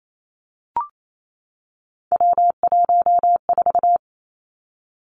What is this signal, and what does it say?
Morse code, heard as a keyed tone, about 28 wpm, 700 Hz. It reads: W14
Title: Three Challenging Letter-Number Combinations 28wpm